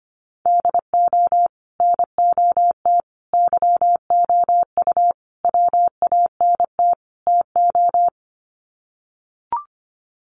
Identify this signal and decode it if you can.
Morse code, heard as a keyed tone, about 25 wpm, 700 Hz. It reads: DO NOT YOU WANT TO
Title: Sets of 5 Words Encore2 25wpm